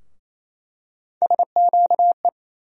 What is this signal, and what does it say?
Morse code, heard as a keyed tone, about 27 wpm, 700 Hz. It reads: SQE
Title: USA Call Sign Postfixes - 3 Letters 28wpm